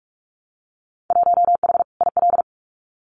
Morse code, heard as a keyed tone, about 44 wpm, 700 Hz. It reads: 1H IL
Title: Winter Field Day Exchanges 45wpm